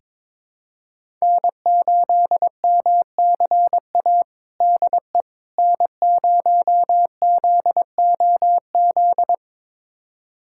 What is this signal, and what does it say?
Morse code, heard as a keyed tone, about 22 wpm, 700 Hz. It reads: N8MCA DE N0ZOZ
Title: USA Callsign DE Callsign - 5 Characters - 22wpm